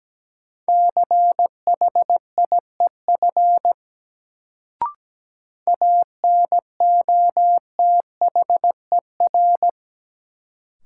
Morse code, heard as a keyed tone, about 17 wpm, 700 Hz. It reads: CHIEF ANOTHER
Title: Sets of 5 Words Encore - New Word Drill - 17wpm